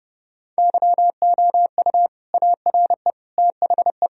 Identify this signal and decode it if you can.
Morse code, heard as a keyed tone, about 30 wpm, 700 Hz. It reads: YOU ARE THE
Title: Sentences from Top 1000 Words - Review Part 4 - 30wpm